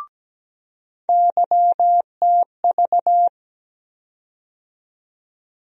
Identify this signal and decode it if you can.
Morse code, heard as a keyed tone, about 17 wpm, 700 Hz. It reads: YTV
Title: USA Call Sign Postfixes - 3 Letters 17wpm